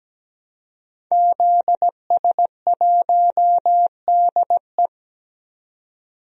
Morse code, heard as a keyed tone, about 17 wpm, 700 Hz. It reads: ZS1DE
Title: DXCC-100 Call Signs - 5 Characters - 17wpm